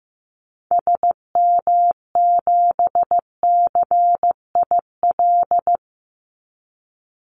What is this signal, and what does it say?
Morse code, heard as a keyed tone, about 15 wpm, 700 Hz. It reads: SM7CIL
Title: Call Signs Repeated 2x - 15wpm